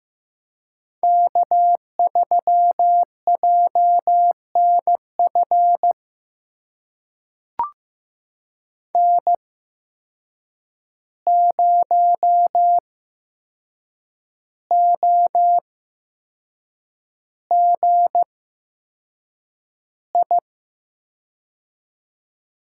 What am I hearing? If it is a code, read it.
Morse code, heard as a keyed tone, about 15 wpm, spaced as for 6 wpm, 700 Hz. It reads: K3JNF N0OGI
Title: USA Call Signs - 5 Characters - 8x Character Spacing - 15wpm